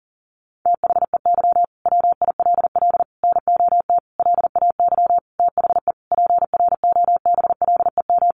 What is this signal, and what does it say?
Morse code, heard as a keyed tone, about 40 wpm, 700 Hz. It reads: THEY WILL NOT LAY THE PROBLEM
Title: Sentences from Top 1500 Words 40wpm